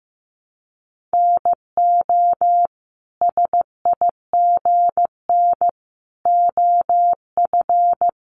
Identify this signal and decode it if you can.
Morse code, heard as a keyed tone, about 15 wpm, 700 Hz. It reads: NO SIGN OF
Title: Sentences from Top 1000 Words 15wpm